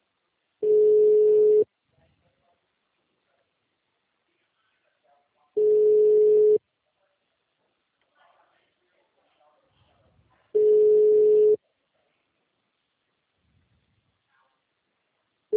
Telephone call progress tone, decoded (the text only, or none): ringback tone